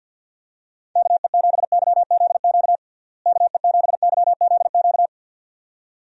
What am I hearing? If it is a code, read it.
Morse code, heard as a keyed tone, about 48 wpm, 700 Hz. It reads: KE7YZQ KE7YZQ
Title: USA Call Signs - 6 Characters - Repeated 2x - 50wpm